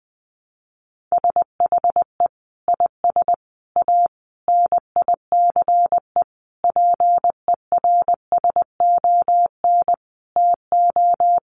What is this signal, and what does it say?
Morse code, heard as a keyed tone, about 20 wpm, 700 Hz. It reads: SHE IS A NICE PERSON TO
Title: Sentences from Top 1100 Words 20wpm